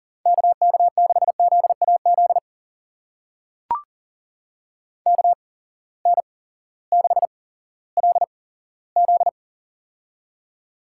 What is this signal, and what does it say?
Morse code, heard as a keyed tone, about 39 wpm, spaced as for 18 wpm, 700 Hz. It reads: KK6ZAZ KN6LZ
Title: USA Call Signs - 8x Character Spacing - 40wpm